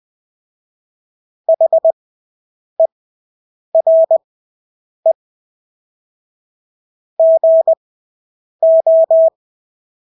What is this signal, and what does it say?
Morse code, heard as a keyed tone, about 20 wpm, spaced as for 8 wpm, 650 Hz. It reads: HERE GO
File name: Engelske ord 20/8 wpm